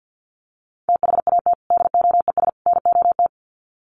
Morse code, heard as a keyed tone, about 49 wpm, 700 Hz. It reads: THAT DOES NOT
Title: Sets of 3 Words 50wpm